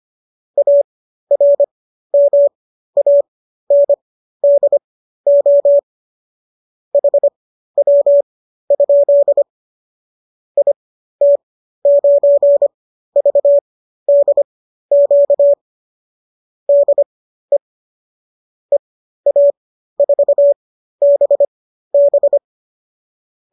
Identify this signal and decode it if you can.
Morse code, heard as a keyed tone, about 25 wpm, spaced as for 13 wpm, 570 Hz. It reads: ARMANDO HW? IT9VDQ DE EA4BB